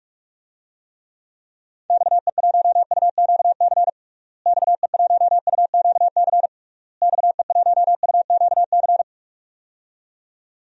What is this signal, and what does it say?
Morse code, heard as a keyed tone, about 46 wpm, 700 Hz. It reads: XE1UQC XE1UQC XE1UQC
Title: DXCC-100 Call Signs - 6 Characters - Repeated 3x - 45wpm